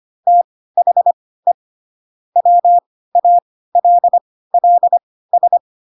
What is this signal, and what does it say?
Morse code, heard as a keyed tone, about 25 wpm, spaced as for 16 wpm, 700 Hz. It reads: THE WALLS
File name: Pinocchio / Pinocchio ch1 10 25 ik1hge